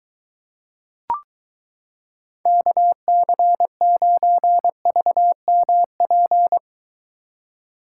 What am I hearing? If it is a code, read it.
Morse code, heard as a keyed tone, about 23 wpm, 700 Hz. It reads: KC9VMP